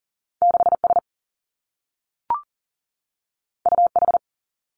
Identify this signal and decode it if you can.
Morse code, heard as a keyed tone, about 40 wpm, 700 Hz. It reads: 6 S U H
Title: Two Challenging Letter-Number Combinations 40wpm